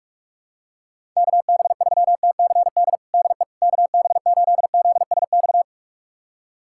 Morse code, heard as a keyed tone, about 46 wpm, 700 Hz. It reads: KB3TCD DE KB8ZSX
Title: USA Callsign DE Callsign - 6 Characters - 45wpm